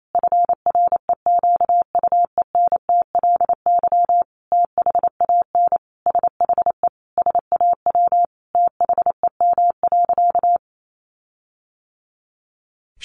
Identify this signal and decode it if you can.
Morse code, heard as a keyed tone, about 28 wpm, 700 Hz. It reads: FREQUENTLY THAN SHE SAW THEM.
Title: Sentences from Top 1300 Words 28wpm